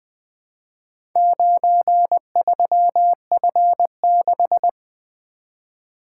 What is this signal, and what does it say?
Morse code, heard as a keyed tone, about 20 wpm, 700 Hz. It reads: 93F6
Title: Four Letter-Number Combinations - 20wpm